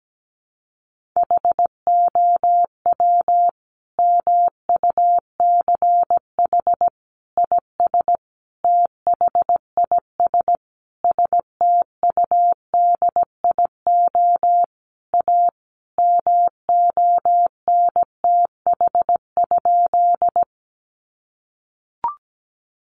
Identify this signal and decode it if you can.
Morse code, heard as a keyed tone, about 17 wpm, 700 Hz. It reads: HOW MUCH IS THIS STUDIO A MONTH?
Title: Sentences from Top 1200 Words 17wpm